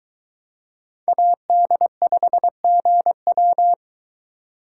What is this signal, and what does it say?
Morse code, heard as a keyed tone, about 20 wpm, 700 Hz. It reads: AD5GW